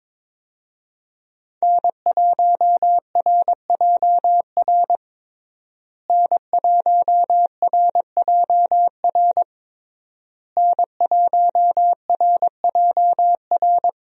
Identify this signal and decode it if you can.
Morse code, heard as a keyed tone, about 21 wpm, 700 Hz. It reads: N1RJR N1RJR N1RJR
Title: USA Call Signs - 5 Characters - Repeated 3x - 22wpm